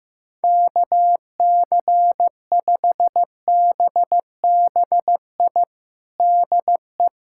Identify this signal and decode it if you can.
Morse code, heard as a keyed tone, about 15 wpm, 700 Hz. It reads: KC5BBI DE
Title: USA Callsign DE Callsign - 6 Characters - 15wpm